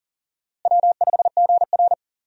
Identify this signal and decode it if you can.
Morse code, heard as a keyed tone, about 40 wpm, 700 Hz. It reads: W5GR
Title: USA Call Signs - 4 Characters - Repeated 3x - 40wpm